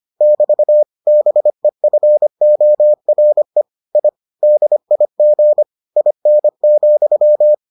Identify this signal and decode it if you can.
Morse code, heard as a keyed tone, about 25 wpm, 600 Hz. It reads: <BT> BEFORE I DIG IN,